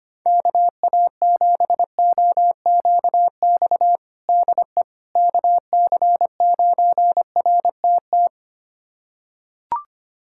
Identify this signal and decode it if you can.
Morse code, heard as a keyed tone, about 25 wpm, 700 Hz. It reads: KA7OQX DE KC9RTT
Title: USA Callsign DE Callsign - 6 Characters - 25wpm